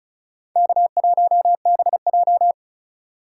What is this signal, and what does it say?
Morse code, heard as a keyed tone, about 36 wpm, 700 Hz. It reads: K1BJ
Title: USA Call Signs - 4 Characters - 2x Character Spacing - 35wpm